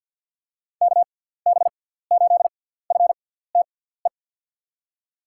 Morse code, heard as a keyed tone, about 51 wpm, spaced as for 17 wpm, 700 Hz. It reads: KB8FTE
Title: USA Call Signs - 6 Characters - 6x Character Spacing - 50wpm